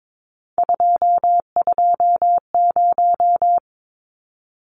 Morse code, heard as a keyed tone, about 22 wpm, 700 Hz. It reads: 220
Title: Numbers - 3-Digit 22wpm